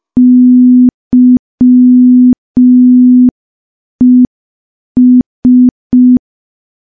Morse code, heard as a keyed tone, about 5 wpm, 250 Hz. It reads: YES
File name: morse / morse-training-05WPM